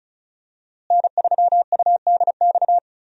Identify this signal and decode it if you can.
Morse code, heard as a keyed tone, about 36 wpm, 700 Hz. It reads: N3UDX
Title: USA Call Signs - 5 Characters - Repeated 3x - 35wpm